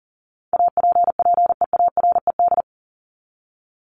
Morse code, heard as a keyed tone, about 40 wpm, 700 Hz. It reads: APPEARED
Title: ED Suffix - Top 100 Words with Base Words 40wpm